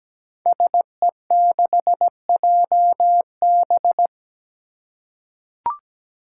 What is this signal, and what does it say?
Morse code, heard as a keyed tone, about 17 wpm, 700 Hz. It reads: SE6JB
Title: Five Challenging Letter-Number Combinations 17wpm